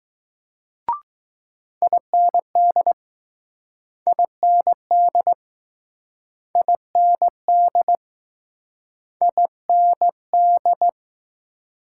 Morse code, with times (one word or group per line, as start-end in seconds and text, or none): 1.8-2.9 IND
4.1-5.3 IND
6.6-8.0 IND
9.2-10.9 IND